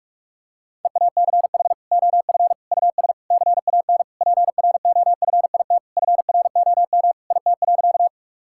Morse code, heard as a keyed tone, about 41 wpm, 700 Hz. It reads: EACH OF US CAN PROFIT FROM IT.
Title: Sentences from Top 1000 Words - Review Part 5 - 45wpm